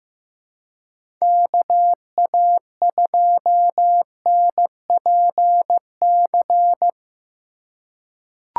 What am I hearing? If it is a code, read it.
Morse code, heard as a keyed tone, about 15 wpm, 700 Hz. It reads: KA2NPC
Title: USA Call Signs - 6 Characters - 2x Character Spacing - 15wpm